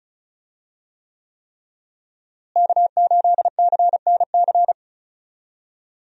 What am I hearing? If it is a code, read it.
Morse code, heard as a keyed tone, about 34 wpm, 700 Hz. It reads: K8CNC
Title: USA Call Signs - 5 Characters - Repeated 3x - 35wpm